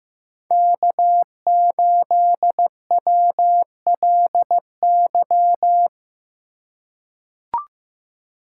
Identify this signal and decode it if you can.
Morse code, heard as a keyed tone, about 15 wpm, 700 Hz. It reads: K8WLY
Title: USA Call Signs - 5 Characters - 2x Character Spacing - 15wpm